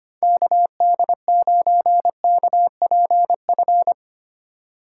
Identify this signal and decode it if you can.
Morse code, heard as a keyed tone, about 25 wpm, 700 Hz. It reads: KD9KPF
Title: USA Call Signs - 6 Characters - 2x Character Spacing - 25wpm